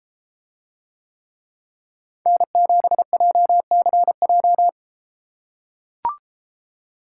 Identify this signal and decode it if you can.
Morse code, heard as a keyed tone, about 33 wpm, 700 Hz. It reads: N7JCJ